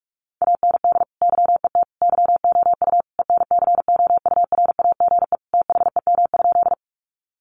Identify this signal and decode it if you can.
Morse code, heard as a keyed tone, about 44 wpm, 700 Hz. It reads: AND YET YOU ENCOURAGE THEM?
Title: Sentences from Top 1400 Words 45wpm